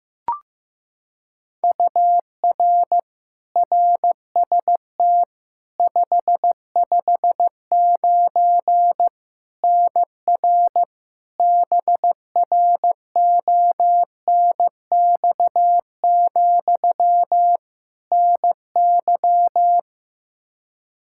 Morse code, heard as a keyed tone, about 15 wpm, 700 Hz. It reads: UR RST 559 NR BRONX, NY
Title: K1USN SST QSO Practice 15wpm